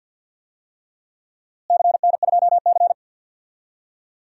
Morse code, heard as a keyed tone, about 52 wpm, 700 Hz. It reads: CN2C
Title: DXCC-100 Call Signs - 4 Characters - Repeated 3x - 50wpm